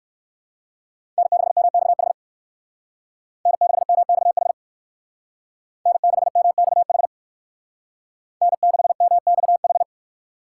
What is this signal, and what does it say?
Morse code, heard as a keyed tone, about 56 wpm, 690 Hz. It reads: N6MXH N6MXH N6MXH N6MXH
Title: Speed Racing - USA 1x3 Call Signs 45wpm